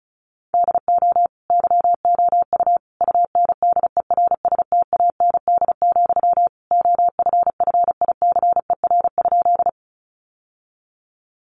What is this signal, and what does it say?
Morse code, heard as a keyed tone, about 34 wpm, 700 Hz. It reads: DO YOU UNDERSTAND, OFFICER?
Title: Sentences from Top 1200 Words 35wpm